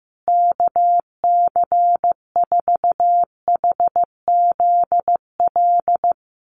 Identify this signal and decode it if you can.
Morse code, heard as a keyed tone, about 15 wpm, 700 Hz. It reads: KC4HZL
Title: USA Call Signs Repeated 3x 15wpm